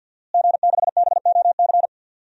Morse code, heard as a keyed tone, about 52 wpm, 700 Hz. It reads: G6BOC
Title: DXCC-100 Call Signs - 5 Characters - Repeated 3x - 50wpm